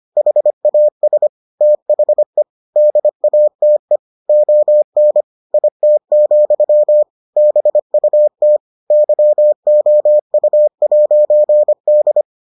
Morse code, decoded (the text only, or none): HAS THE DATE ON IT, BUT YOU'D